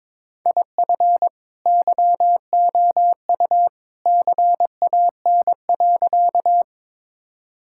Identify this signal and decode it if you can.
Morse code, heard as a keyed tone, about 22 wpm, 700 Hz. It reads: IF YOU CAN.
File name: practice / 22wpm / Sentences with 2-Word Phrases - 22wpm